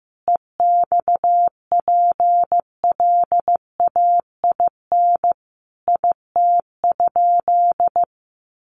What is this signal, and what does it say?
Morse code, heard as a keyed tone, about 15 wpm, 700 Hz. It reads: EXPLAIN IT?
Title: Sentences from Top 1100 Words 15wpm